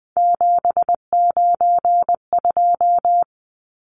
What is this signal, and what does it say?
Morse code, heard as a keyed tone, about 20 wpm, 700 Hz. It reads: 792
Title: Numbers - 3-Digit 20wpm